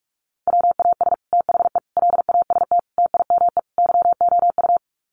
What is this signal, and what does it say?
Morse code, heard as a keyed tone, about 44 wpm, 700 Hz. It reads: WAS THE LAST TIME YOU
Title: Sets of 5 Words Encore2 45wpm